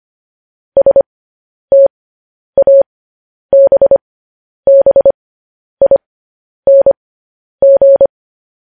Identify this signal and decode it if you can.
Morse code, heard as a keyed tone, about 25 wpm, spaced as for 10 wpm, 550 Hz. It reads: STABBING